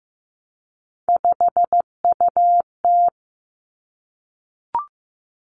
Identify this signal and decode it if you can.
Morse code, heard as a keyed tone, about 15 wpm, 700 Hz. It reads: 5UT
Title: Three Challenging Letter-Number Combinations 15wpm